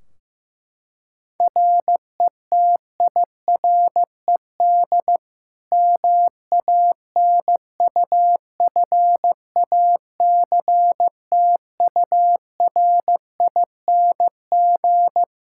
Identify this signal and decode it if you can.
Morse code, heard as a keyed tone, about 16 wpm, 700 Hz. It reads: RETIRED MANUFACTURING
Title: Practice QSO - Exchange 3 - More Info - 15wpm